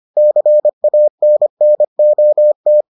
Morse code, heard as a keyed tone, about 25 wpm, 600 Hz. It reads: CANNOT